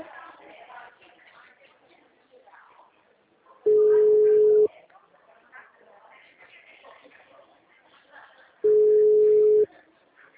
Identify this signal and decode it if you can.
Ringback tone (Europe, 425 Hz), with speech.